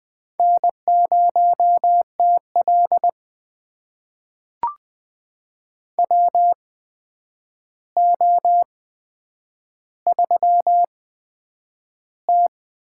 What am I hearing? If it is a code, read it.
Morse code, heard as a keyed tone, about 20 wpm, spaced as for 9 wpm, 700 Hz. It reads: N0TL WO3T
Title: USA Call Signs - 4 Characters - 8x Character Spacing - 20wpm